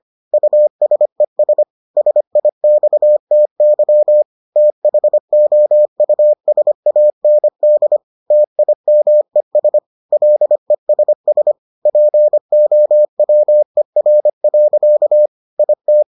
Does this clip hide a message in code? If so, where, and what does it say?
Yes: Morse code from 0.3 to 16.0 s, heard as a keyed tone, about 25 wpm, 600 Hz. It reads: USES SIXTY THOUSAND TIMES LESS POWER. IT